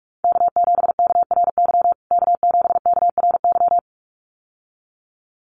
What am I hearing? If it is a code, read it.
Morse code, heard as a keyed tone, about 44 wpm, 700 Hz. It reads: K7KRY K7KRY
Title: USA Call Signs Repeated 2x 45wpm